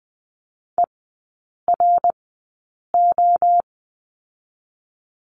Morse code, heard as a keyed tone, about 20 wpm, spaced as for 8 wpm, 700 Hz. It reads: ERO
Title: 053 Lesson 10e - Letter r - Three Characters 20wpm